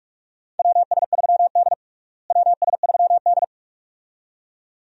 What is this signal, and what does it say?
Morse code, heard as a keyed tone, about 46 wpm, 700 Hz. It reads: WS3D WS3D
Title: USA Call Signs - 4 Characters - Repeated 2x - 45wpm